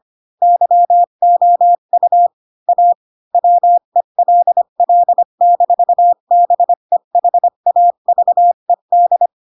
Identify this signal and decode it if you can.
Morse code, heard as a keyed tone, about 25 wpm, 700 Hz. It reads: YOU A WELL-BEHAVED
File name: Pinocchio / Pinocchio ch3 20 25 ik1hge